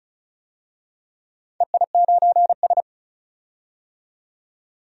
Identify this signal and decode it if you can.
Morse code, heard as a keyed tone, about 36 wpm, 700 Hz. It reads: EI9S
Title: DXCC-100 Call Signs - 4 Characters - 35wpm